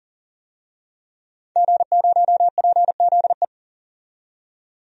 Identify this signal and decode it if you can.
Morse code, heard as a keyed tone, about 40 wpm, 700 Hz. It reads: G0PZE
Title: DXCC-100 Call Signs - 5 Characters - 40wpm